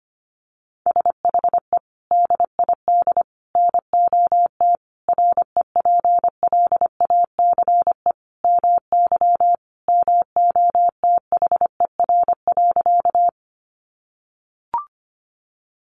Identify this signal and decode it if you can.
Morse code, heard as a keyed tone, about 25 wpm, 700 Hz. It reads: SHE DID NOT REPLACE MY MOTHER.
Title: Sentences from Top 1400 Words 25wpm